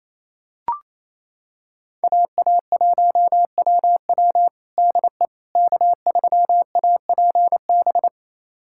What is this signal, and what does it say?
Morse code, heard as a keyed tone, about 28 wpm, 700 Hz. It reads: AA1WW DE K3APB
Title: USA Callsign DE Callsign - 5 Characters - 28wpm